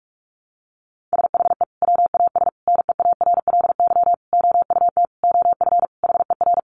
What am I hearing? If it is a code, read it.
Morse code, heard as a keyed tone, about 44 wpm, 700 Hz. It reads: SHE WAS NEARLY OUT OF HER
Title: Sentences from Top 1300 Words 45wpm